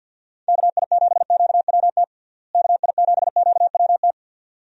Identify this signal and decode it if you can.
Morse code, heard as a keyed tone, about 50 wpm, 700 Hz. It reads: KI7QWT KI7QWT
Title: USA Call Signs - 6 Characters - Repeated 2x - 50wpm